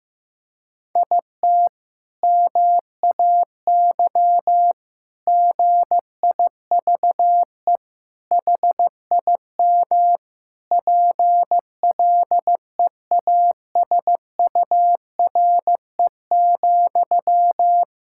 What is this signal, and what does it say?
Morse code, heard as a keyed tone, about 15 wpm, 700 Hz. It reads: IT MAY GIVE HIM PLEASURE,
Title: Sentences from Top 1600 Words 15wpm